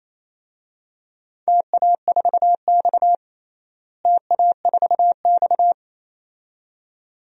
Morse code, heard as a keyed tone, about 28 wpm, 700 Hz. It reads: TA4X TA4X
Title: DXCC-100 Call Signs - 4 Characters - Repeated 2x - 28wpm